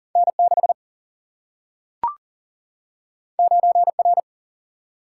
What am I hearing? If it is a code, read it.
Morse code, heard as a keyed tone, about 40 wpm, 700 Hz. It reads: N 6 9 R
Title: Single-Letter Single-Number 40wpm